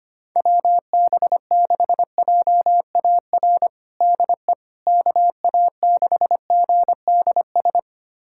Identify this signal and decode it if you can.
Morse code, heard as a keyed tone, about 25 wpm, 700 Hz. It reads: WB6JAR DE KA6GDS
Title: USA Callsign DE Callsign - 6 Characters - 25wpm